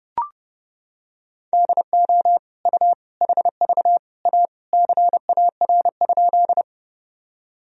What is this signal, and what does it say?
Morse code, heard as a keyed tone, about 30 wpm, 700 Hz. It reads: DO U HV A CAR?
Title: Sentences from Top 300 Words and Abbreviations 30wpm